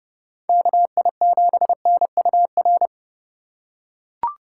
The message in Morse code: KI7NUR

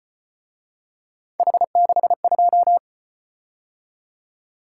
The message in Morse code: H62

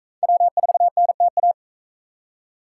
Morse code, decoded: W4NTA

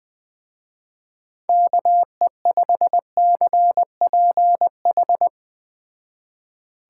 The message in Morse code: KE5CPH

